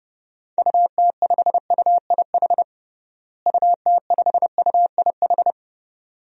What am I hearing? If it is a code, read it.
Morse code, heard as a keyed tone, about 32 wpm, 700 Hz. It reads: UT5UIH UT5UIH
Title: DXCC-100 Call Signs - 6 Characters - Repeated 2x - 30wpm